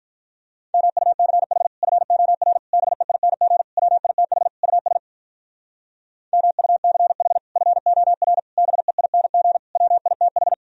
Morse code, heard as a keyed tone, about 43 wpm, 700 Hz. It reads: MUCH FOR BEING WITH US MUCH FOR BEING WITH
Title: Speed-Racing - Sets of 5 Words 35wpm